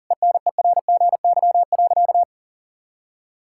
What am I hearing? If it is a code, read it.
Morse code, heard as a keyed tone, about 40 wpm, 700 Hz. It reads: ENERGY.
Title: Sentences from Top 1000 Words - Review Part 6 - 40wpm